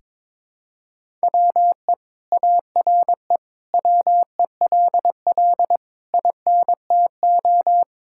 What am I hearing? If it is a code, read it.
Morse code, heard as a keyed tone, about 22 wpm, 700 Hz. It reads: WE ARE WELL INTO
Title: Sentences from Top 1000 Words - Review Part 5 - 22wpm